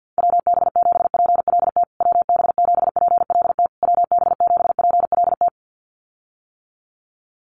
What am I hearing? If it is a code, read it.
Morse code, heard as a keyed tone, about 50 wpm, 700 Hz. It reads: WB7PLT WB7PLT WB7PLT
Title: USA Call Signs Repeated 3x 50wpm